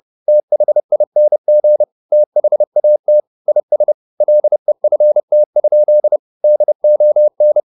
Morse code, heard as a keyed tone, about 30 wpm, 600 Hz. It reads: THING THAT IS LEFT? DON
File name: the old man and the sea / the old man and the sea 1 30wpm